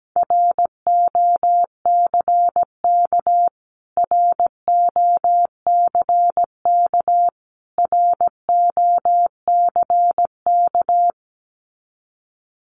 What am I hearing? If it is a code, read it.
Morse code, heard as a keyed tone, about 17 wpm, 700 Hz. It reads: ROCK ROCK ROCK
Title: Top 500 Words Repeated 17wpm